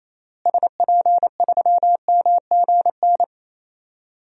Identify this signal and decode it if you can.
Morse code, heard as a keyed tone, about 28 wpm, 700 Hz. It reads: SP3MGN
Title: DXCC-100 Call Signs - 6 Characters - Repeated 3x - 28wpm